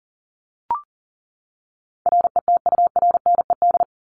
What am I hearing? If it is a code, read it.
Morse code, heard as a keyed tone, about 40 wpm, 700 Hz. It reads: RETURNED
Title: ED Suffix - Top 100 Words with Base Words 40wpm